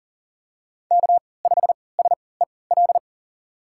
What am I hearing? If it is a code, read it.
Morse code, heard as a keyed tone, about 38 wpm, spaced as for 22 wpm, 690 Hz. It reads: K5SEL
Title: USA Call Signs - 5 Characters - 3x Character Spacing - 40wpm